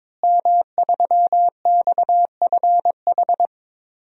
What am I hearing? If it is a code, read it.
Morse code, heard as a keyed tone, about 22 wpm, 700 Hz. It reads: M3XFH
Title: DXCC-100 Call Signs - 5 Characters - Repeated 2x - 22wpm